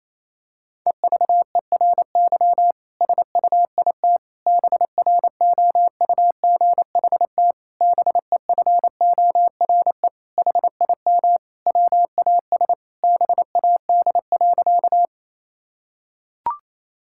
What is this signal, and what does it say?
Morse code, heard as a keyed tone, about 28 wpm, 700 Hz. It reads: EVERY SUIT BROUGHT BEFORE HIM WAS BAD.
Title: Sentences from Top 1600 Words 28wpm